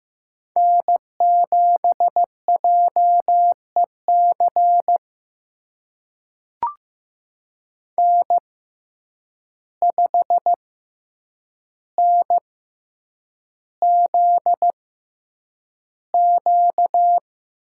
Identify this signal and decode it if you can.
Morse code, heard as a keyed tone, about 15 wpm, spaced as for 8 wpm, 700 Hz. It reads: N7JEC N5NZQ
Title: USA Call Signs - 5 Characters - 6x Character Spacing - 15wpm